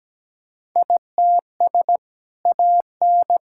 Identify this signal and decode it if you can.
Morse code, heard as a keyed tone, about 17 wpm, 700 Hz. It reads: ITS AN
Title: Sentences from Top 1000 Words - Review Part 6 - 17wpm